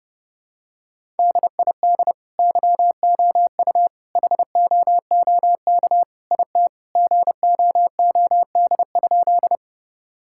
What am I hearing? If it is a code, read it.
Morse code, heard as a keyed tone, about 30 wpm, 700 Hz. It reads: DID YOU HOOK IT GOOD?
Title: Sentences from Top 1600 Words 30wpm